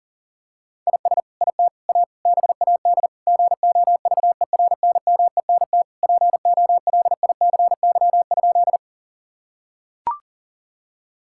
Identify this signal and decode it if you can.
Morse code, heard as a keyed tone, about 37 wpm, 700 Hz. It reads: IS IT A BAD GOVERNMENT POLICY?
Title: Sentences from Top 1000 Words - Review Part 5 - 40wpm